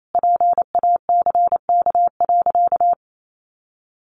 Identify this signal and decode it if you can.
Morse code, heard as a keyed tone, about 28 wpm, 700 Hz. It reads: PACK.
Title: Sentences from Top 1300 Words 28wpm